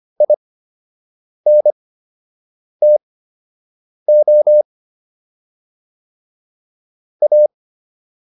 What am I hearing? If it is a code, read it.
Morse code, heard as a keyed tone, about 25 wpm, spaced as for 7 wpm, 600 Hz. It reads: INTO A